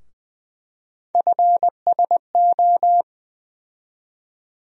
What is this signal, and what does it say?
Morse code, heard as a keyed tone, about 20 wpm, 700 Hz. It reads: FSO
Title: USA Call Sign Postfixes - 3 Letters 20wpm